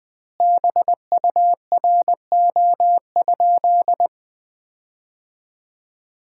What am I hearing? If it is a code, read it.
Morse code, heard as a keyed tone, about 20 wpm, 700 Hz. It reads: BURO?
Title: Common QSO Phrases - 3x Word-Spacing - 20wpm